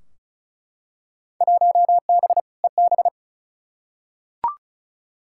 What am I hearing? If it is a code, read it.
Morse code, heard as a keyed tone, about 36 wpm, 700 Hz. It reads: 1B EB
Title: Field Day Exchanges - 35wpm